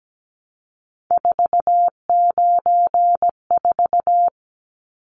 Morse code, heard as a keyed tone, about 17 wpm, 700 Hz. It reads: 494
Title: Numbers - 3-Digit 17wpm